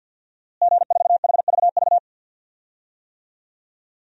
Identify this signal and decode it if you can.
Morse code, heard as a keyed tone, about 46 wpm, 700 Hz. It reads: G4HVV
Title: DXCC-100 Call Signs - 5 Characters - Repeated 3x - 50wpm